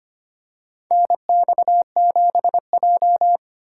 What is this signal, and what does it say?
Morse code, heard as a keyed tone, about 25 wpm, 700 Hz. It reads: NX7J
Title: USA Call Signs - 4 Characters - 3x Character Spacing - 25wpm